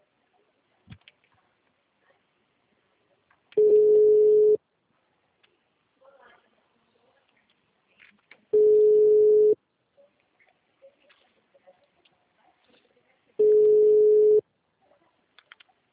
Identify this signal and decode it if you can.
Ringback tone (Europe, 425 Hz).